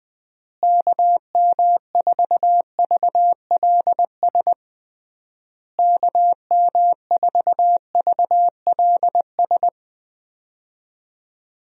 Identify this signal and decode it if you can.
Morse code, heard as a keyed tone, about 20 wpm, 700 Hz. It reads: KM4VLS KM4VLS
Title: USA Call Signs - 6 Characters - Repeated 3x - 20wpm